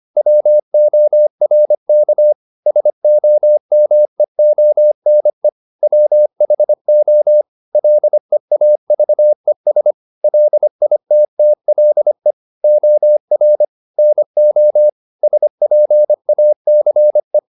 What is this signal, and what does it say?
Morse code, heard as a keyed tone, about 25 wpm, 600 Hz. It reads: WORK SOMEONE WHO LEAVES LITTLE OR NO SPACE